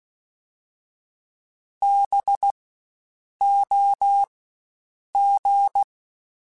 Morse code, heard as a keyed tone, about 16 wpm, spaced as for 7 wpm, 800 Hz. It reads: BOG
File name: Spaced Alphas / Spaced Alphas mp3s / Bs sp @15